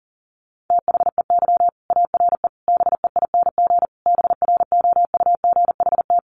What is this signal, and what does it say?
Morse code, heard as a keyed tone, about 40 wpm, 700 Hz. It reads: THEY ARE BEING BROUGHT
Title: Sentences from Top 600 Words 40wpm